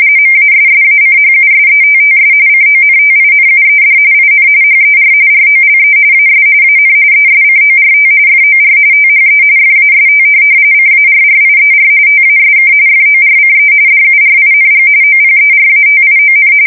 RTTY, about 45.45 baud. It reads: MASSACHUSETTS CONTINUED IN HER ROLE AS AN ESCORT FOR THE FAST CARRIER TASK FORCE DURING OPERATION
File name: fsk170 / USS-Massachusetts